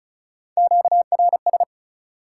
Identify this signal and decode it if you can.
Morse code, heard as a keyed tone, about 36 wpm, 700 Hz. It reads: QRS
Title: Top 180 QSO Elements 35wpm